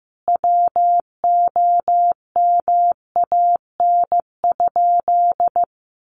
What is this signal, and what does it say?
Morse code, heard as a keyed tone, about 15 wpm, 700 Hz. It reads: WOMAN?
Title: Sentences from Top 1400 Words 15wpm